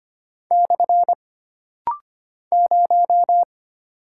Morse code, heard as a keyed tone, about 25 wpm, 700 Hz. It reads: / 0
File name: Single Character (Letter:Number:Punctuation:Prosign) - Rapid-Fire - 25wpm